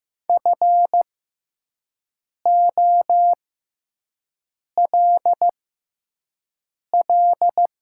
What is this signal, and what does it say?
Morse code, heard as a keyed tone, about 15 wpm, spaced as for 5 wpm, 700 Hz. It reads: FOLL
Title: 6x Character Spacing - Top 200 Words - 15wpm